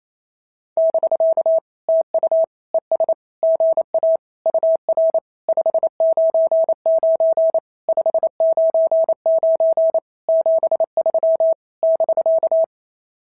Morse code, heard as a keyed tone, about 27 wpm, 650 Hz. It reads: <BK> TU ES GA UR 599 599 73 <BK>
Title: SOTA QSO Practice 28wpm